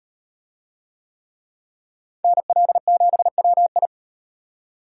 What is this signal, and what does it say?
Morse code, heard as a keyed tone, about 38 wpm, 700 Hz. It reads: NL7WI